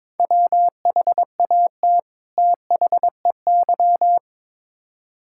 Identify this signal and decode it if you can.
Morse code, heard as a keyed tone, about 22 wpm, 700 Hz. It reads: WHAT THEY
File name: Build A Phrase - 2-Word Phrases - 22wpm